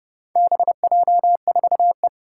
Morse code, heard as a keyed tone, about 30 wpm, 700 Hz. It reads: BJ4E
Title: Four Challenging Letter-Number Combinations 30wpm